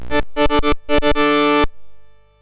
There is no code.